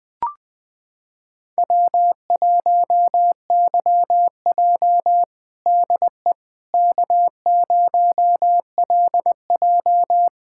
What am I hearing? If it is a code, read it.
Morse code, heard as a keyed tone, about 20 wpm, 700 Hz. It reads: W1YJ DE K0LJ
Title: USA Callsign DE Callsign - 4 Characters - 20wpm